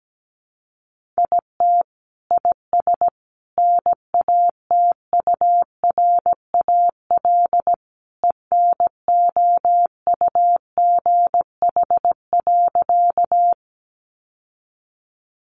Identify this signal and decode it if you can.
Morse code, heard as a keyed tone, about 17 wpm, 700 Hz. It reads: IT IS NATURAL ENOUGH.
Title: Sentences from Top 700 Words 17wpm